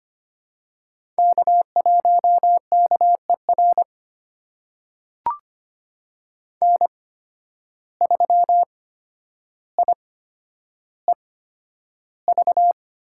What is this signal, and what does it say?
Morse code, heard as a keyed tone, about 25 wpm, spaced as for 11 wpm, 700 Hz. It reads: K1KER N3IEV
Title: USA Call Signs - 5 Characters - 8x Character Spacing - 25wpm